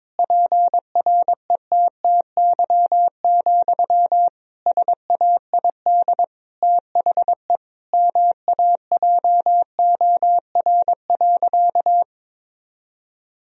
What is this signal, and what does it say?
Morse code, heard as a keyed tone, about 21 wpm, 700 Hz. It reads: PRETTY, SAID THE MAJOR.
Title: Sentences from Top 1000 Words - Review Part 4 - 22wpm